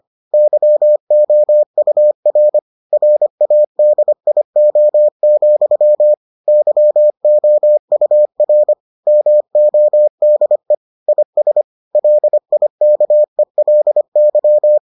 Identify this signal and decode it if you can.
Morse code, heard as a keyed tone, about 25 wpm, 600 Hz. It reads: YOUR RADIO, YOUR MODE IS LIKELY